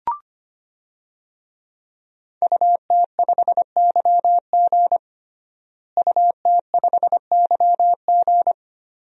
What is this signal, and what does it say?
Morse code, heard as a keyed tone, about 25 wpm, 700 Hz. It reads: UT5YG UT5YG
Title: DXCC-100 Call Signs - 5 Characters - Repeated 3x - 25wpm